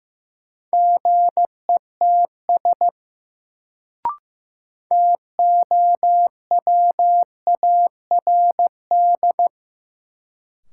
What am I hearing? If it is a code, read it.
Morse code, heard as a keyed tone, about 15 wpm, 700 Hz. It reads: GETS TOWARD
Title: Sentences from Top 1300 Words - New Word Drill Alt - 15wpm